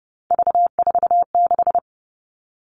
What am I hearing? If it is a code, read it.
Morse code, heard as a keyed tone, about 30 wpm, 700 Hz. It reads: V46
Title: All Cut Numbers - 3-Digit 30wpm